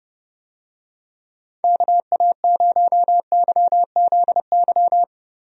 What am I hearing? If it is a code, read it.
Morse code, heard as a keyed tone, about 30 wpm, 700 Hz. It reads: KA0YZY
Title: USA Call Signs - 6 Characters - Repeated 3x - 30wpm